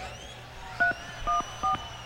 DTMF, heard as telephone touch tones, tones of about 120 ms. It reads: 314